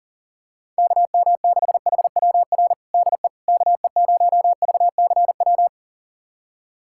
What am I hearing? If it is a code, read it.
Morse code, heard as a keyed tone, about 37 wpm, 700 Hz. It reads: KM6HWR DE KE0VCW
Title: USA Callsign DE Callsign - 6 Characters - 40wpm